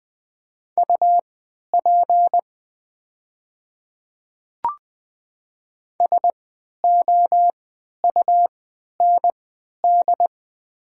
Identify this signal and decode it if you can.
Morse code, heard as a keyed tone, about 20 wpm, spaced as for 11 wpm, 700 Hz. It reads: UP SOUND
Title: 3x Character Spacing - Top 100 Words - 20wpm